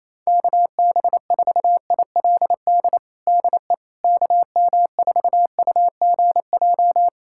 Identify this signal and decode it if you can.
Morse code, heard as a keyed tone, about 28 wpm, 700 Hz. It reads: KB4ILD DE KM4UGJ
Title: USA Callsign DE Callsign - 6 Characters - 28wpm